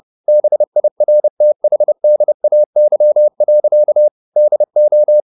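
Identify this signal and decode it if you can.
Morse code, heard as a keyed tone, about 30 wpm, 600 Hz. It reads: BIRTHDAY. DO
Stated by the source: Morse code at 30 wpm, the novel Never by Ken Follett in English, from CW med Gnister.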